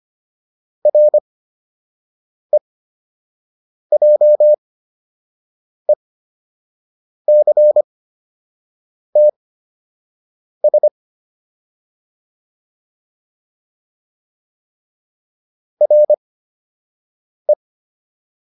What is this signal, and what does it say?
Morse code, heard as a keyed tone, about 25 wpm, spaced as for 6 wpm, 600 Hz. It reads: REJECTS RE
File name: RE7 06